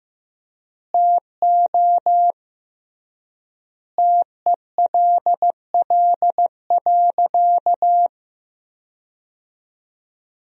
Morse code, heard as a keyed tone, about 15 wpm, 700 Hz. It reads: TO TELL.
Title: 3x Word-Spacing - Sentences from Top 200 Words - 15wpm